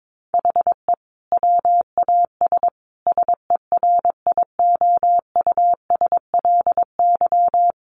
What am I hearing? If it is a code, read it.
Morse code, heard as a keyed tone, about 22 wpm, 700 Hz. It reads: HE WAS SERIOUSLY
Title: Sentences from Top 1200 Words 22wpm